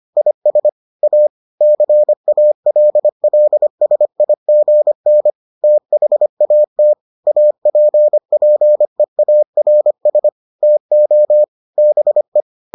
Morse code, heard as a keyed tone, about 25 wpm, 600 Hz. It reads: IS A CALLSIGN THAT APPEARS TO BE